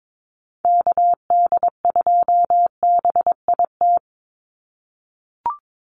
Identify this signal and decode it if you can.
Morse code, heard as a keyed tone, about 22 wpm, 700 Hz. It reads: KD2BIT
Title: USA Call Signs 22wpm